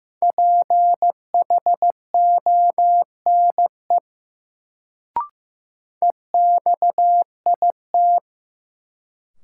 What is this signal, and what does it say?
Morse code, heard as a keyed tone, about 15 wpm, 700 Hz. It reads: PHONE EXIT
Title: Sentences from Top 1100 Words - New Word Drill - 15wpm